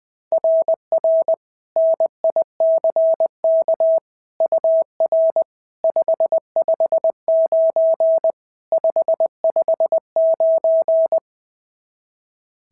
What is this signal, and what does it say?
Morse code, heard as a keyed tone, about 20 wpm, 690 Hz. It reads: RR NICK UR 559 559
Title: SOTA QSO Practice 20wpm